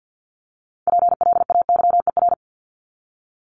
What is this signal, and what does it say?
Morse code, heard as a keyed tone, about 50 wpm, 700 Hz. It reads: PLAYER
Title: Top 501-750 Words 50wpm